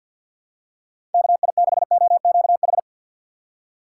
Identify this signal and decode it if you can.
Morse code, heard as a keyed tone, about 49 wpm, 700 Hz. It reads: KI6OQH
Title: USA Call Signs - 6 Characters - 3x Character Spacing - 50wpm